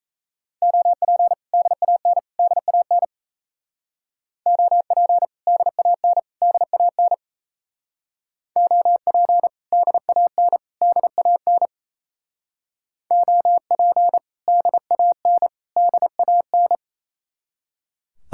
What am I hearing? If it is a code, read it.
Morse code, heard as a keyed tone, about 36 wpm, 700 Hz. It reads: OP DAN DAN OP DAN DAN OP DAN DAN OP DAN DAN